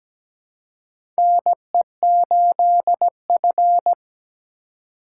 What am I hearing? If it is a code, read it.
Morse code, heard as a keyed tone, about 17 wpm, 700 Hz. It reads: NE8F
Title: USA Call Signs - 4 Characters - 2x Character Spacing - 17wpm